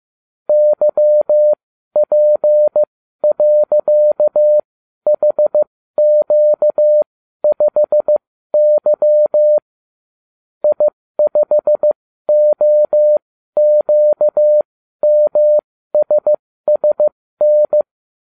Morse code, heard as a keyed tone, about 15 wpm, spaced as for 12 wpm, 600 Hz. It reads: YP.HQ5Y I5OQMSSN